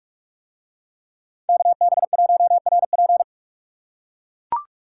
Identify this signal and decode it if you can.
Morse code, heard as a keyed tone, about 44 wpm, 700 Hz. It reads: KB1RP